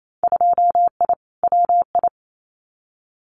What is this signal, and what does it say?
Morse code, heard as a keyed tone, about 28 wpm, 700 Hz. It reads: 2I WI
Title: Winter Field Day Exchanges 28wpm